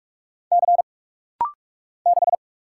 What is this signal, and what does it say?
Morse code, heard as a keyed tone, about 44 wpm, 700 Hz. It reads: C 6
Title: Single Letter-Number Alt - Rapid-Fire 45wpm